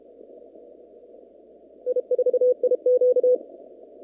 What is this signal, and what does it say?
Morse code, heard as a keyed tone, about 32 wpm, 500 Hz. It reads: I4IQ
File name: CW / kt5j wpx cw-op-k5pi